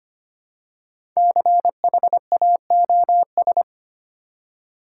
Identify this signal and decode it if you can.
Morse code, heard as a keyed tone, about 25 wpm, 700 Hz. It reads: CHAOS
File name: 5 Letter Words - First 3 Letters Given - 25wpm